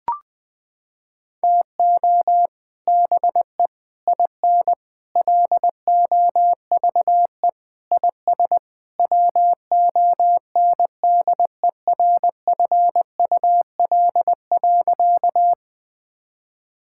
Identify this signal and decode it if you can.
Morse code, heard as a keyed tone, about 21 wpm, 700 Hz. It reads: TO BE IN LOVE IS WONDERFUL.
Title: Sentences with 3-Word Phrases - 20wpm